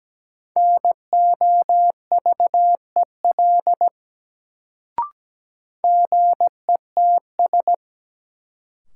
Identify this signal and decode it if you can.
Morse code, heard as a keyed tone, about 17 wpm, 700 Hz. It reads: NOVEL GETS
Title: Sentences from Top 1300 Words - New Word Drill - 17wpm